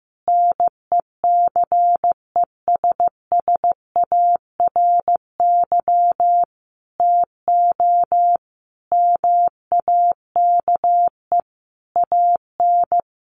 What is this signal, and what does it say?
Morse code, heard as a keyed tone, about 14 wpm, 700 Hz. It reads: NECESSARY TO MAKE AN
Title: Sentences from Top 1200 Words 15wpm